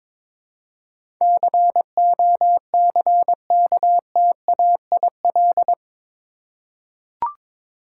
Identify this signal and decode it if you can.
Morse code, heard as a keyed tone, about 21 wpm, 700 Hz. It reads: COCKTAIL
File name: practice / 22wpm / Eight Letter Words - 22wpm